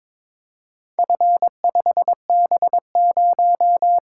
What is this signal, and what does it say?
Morse code, heard as a keyed tone, about 23 wpm, 700 Hz. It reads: F5B0
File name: practice / 22wpm / Four Letter-Number Combinations - 22wpm